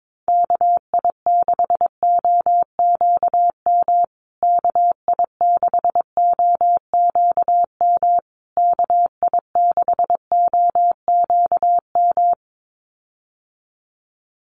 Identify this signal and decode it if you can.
Morse code, heard as a keyed tone, about 22 wpm, 700 Hz. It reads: KI6OQM KI6OQM KI6OQM
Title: USA Call Signs Repeated 3x 22wpm